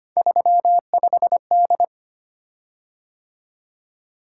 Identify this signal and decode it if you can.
Morse code, heard as a keyed tone, about 25 wpm, 700 Hz. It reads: 35D
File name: Three Letter-Number Combinations - 25wpm